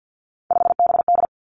Morse code, heard as a keyed tone, about 50 wpm, 700 Hz. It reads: 5BD
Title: Three Challenging Letter-Number Combinations 50wpm